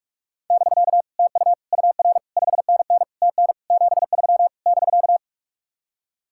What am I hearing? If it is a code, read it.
Morse code, heard as a keyed tone, about 50 wpm, 700 Hz. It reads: <BK> TU UR 5NN TN 73 <BK>